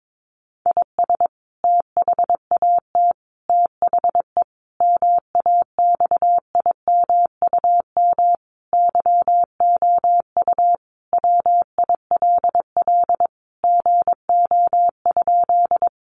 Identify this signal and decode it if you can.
Morse code, heard as a keyed tone, about 22 wpm, 700 Hz. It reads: IS THAT THE MAXIMUM YOU WILL GO?
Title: Sentences from Top 1300 Words 22wpm